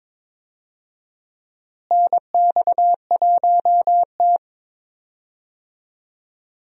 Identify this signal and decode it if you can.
Morse code, heard as a keyed tone, about 22 wpm, 700 Hz. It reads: NX1T
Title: USA Call Signs - 4 Characters - Repeated 3x - 22wpm